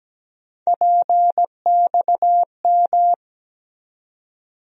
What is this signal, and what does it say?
Morse code, heard as a keyed tone, about 17 wpm, 700 Hz. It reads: PXM
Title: USA Call Sign Postfixes - 3 Letters 17wpm